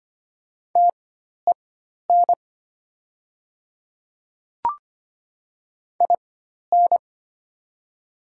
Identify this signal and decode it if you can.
Morse code, heard as a keyed tone, about 25 wpm, spaced as for 12 wpm, 690 Hz. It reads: TEN IN
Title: 4x Character Spacing - Top 500 Words - 25wpm